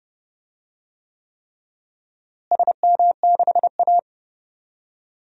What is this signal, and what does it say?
Morse code, heard as a keyed tone, about 30 wpm, 700 Hz. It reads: SM6A
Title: DXCC-100 Call Signs - 4 Characters - Repeated 3x - 30wpm